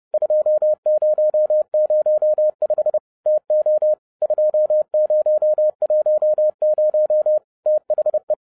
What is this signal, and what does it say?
Morse code, heard as a keyed tone, about 30 wpm, 610 Hz. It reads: 2005 TO 2010 THE
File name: 30wpm